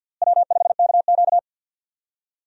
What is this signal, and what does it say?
Morse code, heard as a keyed tone, about 49 wpm, 700 Hz. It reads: W5KQ
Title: USA Call Signs - 4 Characters - Repeated 3x - 50wpm